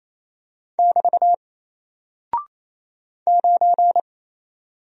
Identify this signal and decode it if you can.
Morse code, heard as a keyed tone, about 28 wpm, 700 Hz. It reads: <BT> 9
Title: Single Character (Letter:Number:Punctuation:Prosign) Alt 28wpm